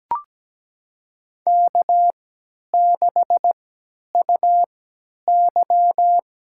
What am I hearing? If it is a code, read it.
Morse code, heard as a keyed tone, about 17 wpm, spaced as for 10 wpm, 700 Hz. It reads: K6UY
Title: USA Call Signs - 4 Characters - 3x Character Spacing - 17wpm